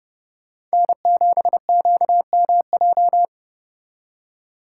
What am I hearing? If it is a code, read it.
Morse code, heard as a keyed tone, about 35 wpm, 700 Hz. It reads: N7QMJ